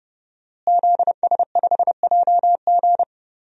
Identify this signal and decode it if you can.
Morse code, heard as a keyed tone, about 30 wpm, 700 Hz. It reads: ZS5JG
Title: DXCC-100 Call Signs - 5 Characters - Repeated 3x - 30wpm